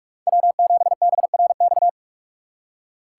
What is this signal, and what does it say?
Morse code, heard as a keyed tone, about 43 wpm, 700 Hz. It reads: W7BRX